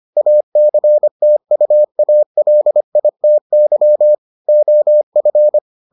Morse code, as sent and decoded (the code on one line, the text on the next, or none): .- -.-. - ..- .- .-.. .. - -.-- / --- ..-.
ACTUALITY OF